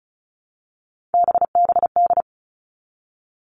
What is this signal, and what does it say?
Morse code, heard as a keyed tone, about 34 wpm, 700 Hz. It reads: BBD
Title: All Cut Numbers - 3-Digit 35wpm